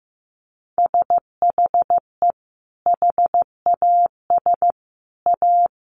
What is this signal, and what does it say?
Morse code, heard as a keyed tone, about 14 wpm, 700 Hz. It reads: SHE HAS A
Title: Sentences from Top 600 Words 15wpm